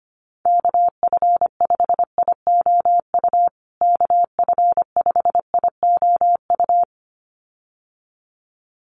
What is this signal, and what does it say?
Morse code, heard as a keyed tone, about 25 wpm, 700 Hz. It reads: KF5IOU KF5IOU
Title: USA Call Signs Repeated 2x 25wpm